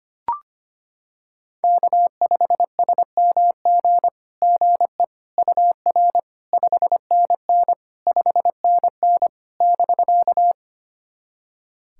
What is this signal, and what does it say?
Morse code, heard as a keyed tone, about 25 wpm, 700 Hz. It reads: K5SMG GE UR 5NN 5NN <BK>
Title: SOTA QSO Practice 25wpm